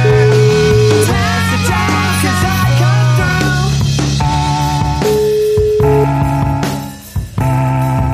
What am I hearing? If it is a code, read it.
Ringback tone (Europe, 425 Hz), with noise.